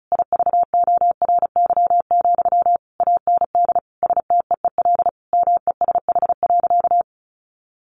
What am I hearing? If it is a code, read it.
Morse code, heard as a keyed tone, about 34 wpm, 700 Hz. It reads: IVORY, AND STEEL MESH.
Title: A Sound Of Thunder 35wpm